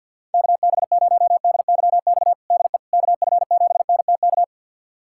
Morse code, heard as a keyed tone, about 52 wpm, 690 Hz. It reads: KB0DYX DE KF7NTK
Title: USA Callsign DE Callsign - 6 Characters - 50wpm